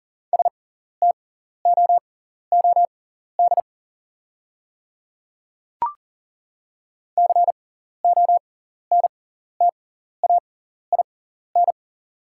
Morse code, heard as a keyed tone, about 38 wpm, spaced as for 14 wpm, 700 Hz. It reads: STOOD CONTAIN
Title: 6x Character Spacing - Top 500 Words - 40wpm